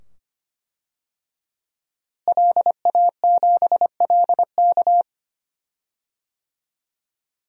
Morse code, heard as a keyed tone, about 25 wpm, 700 Hz. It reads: LA7LK
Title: DXCC-100 Call Signs - 5 Characters - Repeated 2x - 25wpm